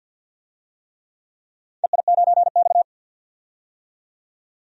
Morse code, heard as a keyed tone, about 53 wpm, 700 Hz. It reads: EI9X
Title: DXCC-100 Call Signs - 4 Characters - 50wpm